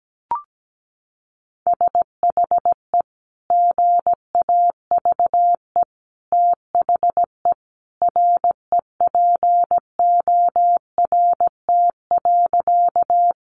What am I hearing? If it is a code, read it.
Morse code, heard as a keyed tone, about 17 wpm, 700 Hz. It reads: SHE GAVE THE REPORT.
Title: Sentences from Top 900 Words 17wpm